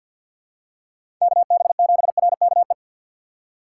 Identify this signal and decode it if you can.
Morse code, heard as a keyed tone, about 51 wpm, 700 Hz. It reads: KB7RKE
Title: USA Call Signs - 2x Character Spacing - 50wpm